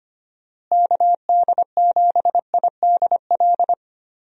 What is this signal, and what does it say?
Morse code, heard as a keyed tone, about 25 wpm, 700 Hz. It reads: KD7IDL
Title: USA Call Signs - 6 Characters - Repeated 2x - 25wpm